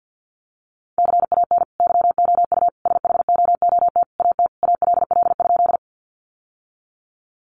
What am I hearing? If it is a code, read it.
Morse code, heard as a keyed tone, about 50 wpm, 700 Hz. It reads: CAN YOU SHOOT AT ALL?
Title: Sentences from Top 1200 Words 50wpm